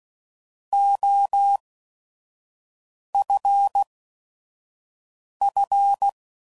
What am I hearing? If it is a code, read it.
Morse code, heard as a keyed tone, about 16 wpm, spaced as for 5 wpm, 800 Hz. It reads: OFF
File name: Prefixes / By First Letter / Os